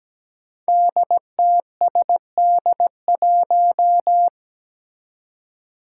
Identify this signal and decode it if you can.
Morse code, heard as a keyed tone, about 17 wpm, 700 Hz. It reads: DTSD1
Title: Five Challenging Letter-Number Combinations 17wpm